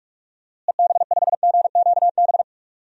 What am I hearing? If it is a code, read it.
Morse code, heard as a keyed tone, about 43 wpm, 700 Hz. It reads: EB5GQB
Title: DXCC-100 Call Signs - 6 Characters - Repeated 3x - 45wpm